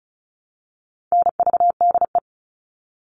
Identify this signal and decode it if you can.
Morse code, heard as a keyed tone, about 34 wpm, 700 Hz. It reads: NVDE
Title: All Cut Numbers - 4-Digit 35wpm